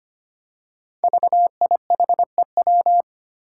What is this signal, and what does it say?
Morse code, heard as a keyed tone, about 25 wpm, 700 Hz. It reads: VIHEW
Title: Code Groups - 5 Letter:Numbers - 25wpm